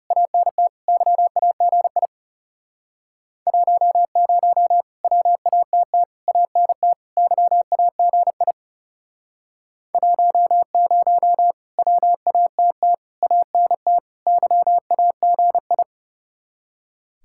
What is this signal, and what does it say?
Morse code, heard as a keyed tone, about 40 wpm, 700 Hz. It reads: ANT YAGI 10 WATT ANT YAGI 10 WATT ANT YAGI